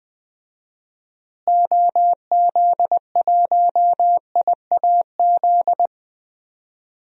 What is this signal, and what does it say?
Morse code, heard as a keyed tone, about 20 wpm, 700 Hz. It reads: OZ1IAZ
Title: DXCC-100 Call Signs - 6 Characters - Repeated 2x - 20wpm